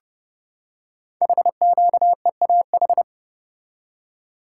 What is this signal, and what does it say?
Morse code, heard as a keyed tone, about 30 wpm, 700 Hz. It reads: HQEAH
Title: Code Groups - 5 Letters - 30wpm